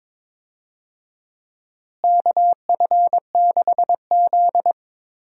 Morse code, heard as a keyed tone, about 22 wpm, 700 Hz. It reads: KF6Z
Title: USA Call Signs - 4 Characters - Repeated 2x - 22wpm